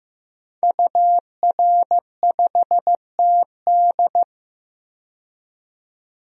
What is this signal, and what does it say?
Morse code, heard as a keyed tone, about 15 wpm, 700 Hz. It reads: UR5TD
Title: DXCC-100 Call Signs - 5 Characters - 15wpm